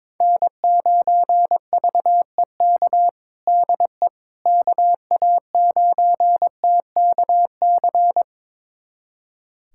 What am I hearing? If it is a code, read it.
Morse code, heard as a keyed tone, about 22 wpm, 700 Hz. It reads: N9VEK DE KA9TKC
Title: USA Callsign DE Callsign 22wpm